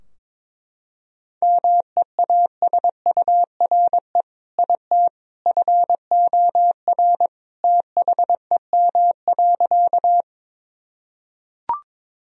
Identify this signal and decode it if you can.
Morse code, heard as a keyed tone, about 21 wpm, 700 Hz. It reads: MEASURE IT FOR THEM.
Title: Sentences from Top 1000 Words - Review Part 2 - 22wpm